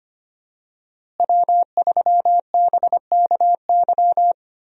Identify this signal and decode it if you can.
Morse code, heard as a keyed tone, about 25 wpm, 700 Hz. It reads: W3BKY
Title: USA Call Signs - 5 Characters - Repeated 2x - 25wpm